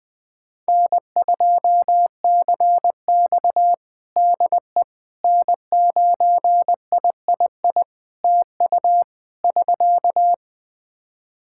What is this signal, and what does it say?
Morse code, heard as a keyed tone, about 20 wpm, 700 Hz. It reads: N2CX DE N9III TU <SK>
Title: K1USN SST QSO Practice 20wpm